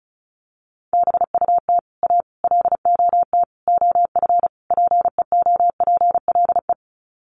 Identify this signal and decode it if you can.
Morse code, heard as a keyed tone, about 34 wpm, 700 Hz. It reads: BUT A LOT OF PEOPLE
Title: Sets of 5 Words Encore2 35wpm